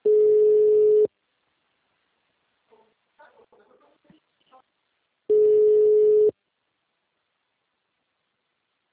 Ringback tone (Europe, 425 Hz).